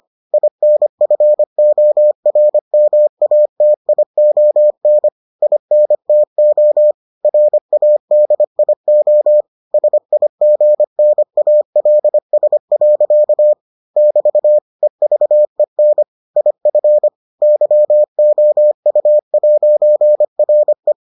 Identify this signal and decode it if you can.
Morse code, heard as a keyed tone, about 25 wpm, 600 Hz. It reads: INFORMATION INTO RADIO SIGNALS. <BT> EVEN IF YOU'RE